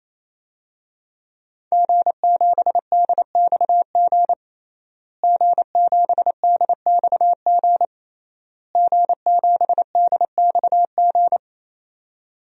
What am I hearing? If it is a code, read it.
Morse code, heard as a keyed tone, about 28 wpm, 700 Hz. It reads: G7DXG G7DXG G7DXG
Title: DXCC-100 Call Signs - 5 Characters - Repeated 3x - 28wpm